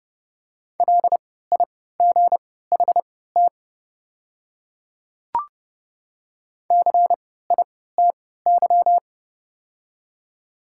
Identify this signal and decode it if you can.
Morse code, heard as a keyed tone, about 30 wpm, spaced as for 17 wpm, 700 Hz. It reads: LIGHT CITY
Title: 3x Character Spacing - Top 300 Words - 30wpm